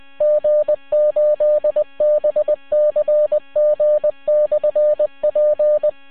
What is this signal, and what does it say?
Morse code, heard as a keyed tone, about 20 wpm, 620 Hz. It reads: G8BCG/P